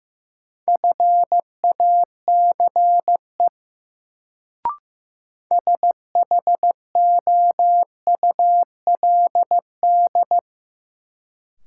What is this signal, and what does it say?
Morse code, heard as a keyed tone, about 15 wpm, 700 Hz. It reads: FACE SHOULD
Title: Sentences from Top 300 Words - New Word Drill - 15wpm